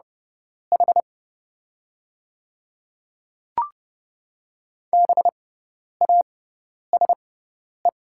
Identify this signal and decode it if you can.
Morse code, heard as a keyed tone, about 29 wpm, spaced as for 10 wpm, 690 Hz. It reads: H BASE
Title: 6x Character Spacing - Top 300 Words - 30wpm